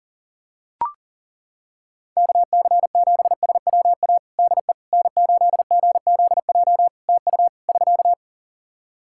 Morse code, heard as a keyed tone, about 40 wpm, 700 Hz. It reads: KC7SWA DE N8GZJ TU <SK>
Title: Practice QSO - Exchange 4 - Ending - 40wpm